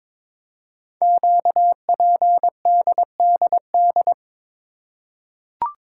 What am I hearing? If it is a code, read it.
Morse code, heard as a keyed tone, about 22 wpm, 700 Hz. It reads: QPDDD
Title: Code Groups - 5 Letters - 22wpm